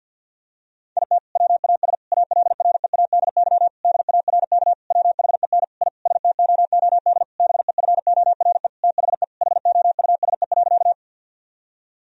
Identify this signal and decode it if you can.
Morse code, heard as a keyed tone, about 52 wpm, 700 Hz. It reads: IT WAS ALREADY DARK WHEN I STOOD BEFORE THE HOUSE.
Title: Sentences from Top 1000 Words - Review Part 2 - 50wpm